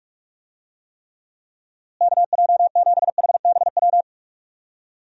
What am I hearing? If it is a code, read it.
Morse code, heard as a keyed tone, about 45 wpm, 700 Hz. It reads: KJ7HBW
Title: USA Call Signs - 6 Characters - Repeated 3x - 45wpm